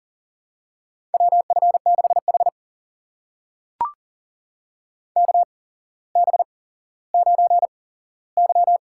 Morse code, heard as a keyed tone, about 40 wpm, spaced as for 17 wpm, 700 Hz. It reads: WF6H KB9Y
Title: USA Call Signs - 4 Characters - 8x Character Spacing - 40wpm